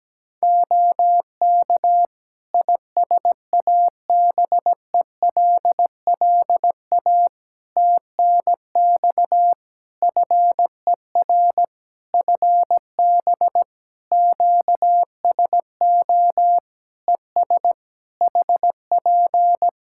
Morse code, heard as a keyed tone, about 17 wpm, 700 Hz. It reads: OK ISABELLA TNX FER FB QSO ES HP
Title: Practice QSO - Exchange 4 - Ending - 17wpm